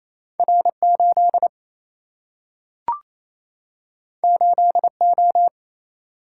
Morse code, heard as a keyed tone, about 28 wpm, 700 Hz. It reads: R 8 8 O
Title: Single-Letter Single-Number 28wpm